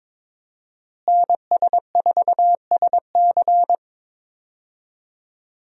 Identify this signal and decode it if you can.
Morse code, heard as a keyed tone, about 22 wpm, 700 Hz. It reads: NS4SC
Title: USA Call Signs - 5 Characters - Repeated 2x - 22wpm